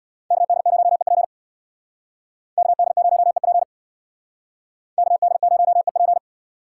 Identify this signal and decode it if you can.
Morse code, heard as a keyed tone, about 67 wpm, 700 Hz. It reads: KD0EP KD0EP KD0EP